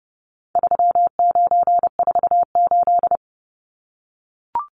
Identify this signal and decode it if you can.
Morse code, heard as a keyed tone, about 30 wpm, 700 Hz. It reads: 3948
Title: Numbers - 4-Digit 30wpm